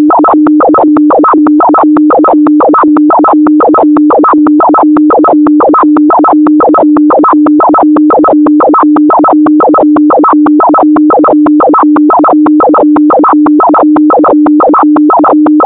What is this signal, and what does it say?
Two-tone teleprinter signal, about 400 Hz shift.
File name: audio notizie / radia s24 n297 radiohelsinki raggam dataradio pt4